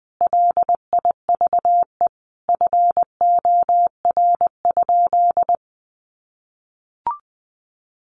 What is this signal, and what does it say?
Morse code, heard as a keyed tone, about 20 wpm, 700 Hz. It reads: LIVE FOR?
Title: Sentences from Top 200 Words 20wpm